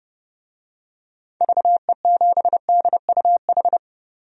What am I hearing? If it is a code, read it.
Morse code, heard as a keyed tone, about 30 wpm, 700 Hz. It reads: VE7DUH
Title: DXCC-100 Call Signs - 6 Characters - 30wpm